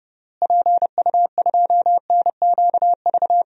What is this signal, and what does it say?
Morse code, heard as a keyed tone, about 30 wpm, 700 Hz. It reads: PU2NQV
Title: DXCC-100 Call Signs - 6 Characters - Repeated 3x - 30wpm